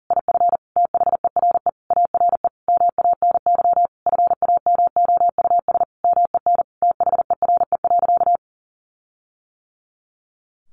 Morse code, heard as a keyed tone, about 40 wpm, 700 Hz. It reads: IF THERE ARE MANY FAMOUS MEN THERE.
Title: Sentences from Top 1500 Words 40wpm